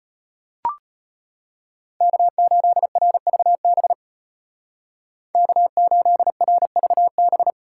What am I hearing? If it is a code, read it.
Morse code, heard as a keyed tone, about 33 wpm, 700 Hz. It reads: K8RVB K8RVB